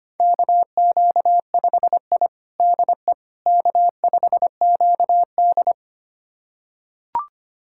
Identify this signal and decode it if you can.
Morse code, heard as a keyed tone, about 25 wpm, 700 Hz. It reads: KQ5I DE K5QD
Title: USA Callsign DE Callsign - 4 Characters - 25wpm